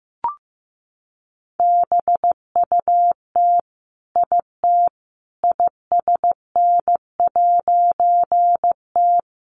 Morse code, heard as a keyed tone, about 15 wpm, 700 Hz. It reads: BUT IT ISN'T
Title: A Sound Of Thunder 15wpm